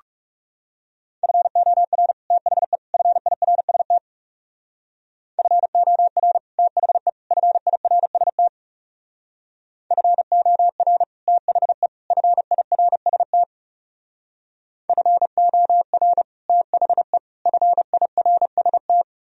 Morse code, heard as a keyed tone, about 36 wpm, 700 Hz. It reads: FOR THE FIRST FOR THE FIRST FOR THE FIRST FOR THE FIRST
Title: Speed-Racing - Sets of 3 Words 30wpm